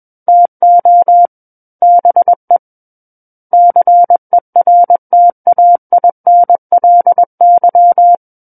Morse code, heard as a keyed tone, about 21 wpm, 700 Hz. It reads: TO BE CERTAINLY
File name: Book One / 12-Chapter 12 - 21 WPM